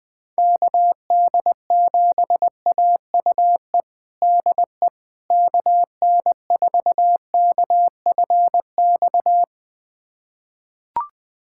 Morse code, heard as a keyed tone, about 20 wpm, 700 Hz. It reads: KD7AUE DE KN4KFX
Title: USA Callsign DE Callsign - 6 Characters - 20wpm